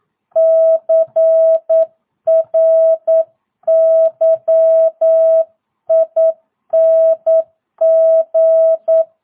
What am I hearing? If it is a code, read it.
Morse code, heard as a keyed tone, about 9 wpm, 660 Hz. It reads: CRYING
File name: jh8cbh / 2022cw